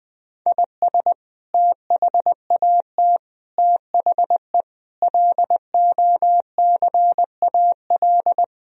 Morse code, heard as a keyed tone, about 21 wpm, 700 Hz. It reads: IS THAT THE LOCAL
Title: Sentences from Top 1000 Words - Review Part 3 - 20wpm